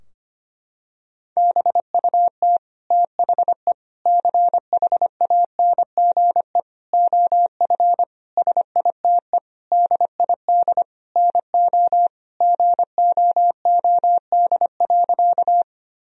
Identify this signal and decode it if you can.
Morse code, heard as a keyed tone, about 25 wpm, 700 Hz. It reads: BUT THE CHANGE OF SITE DID NO GOOD.
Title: Sentences from Top 1000 Words - Review Part 4 - 25wpm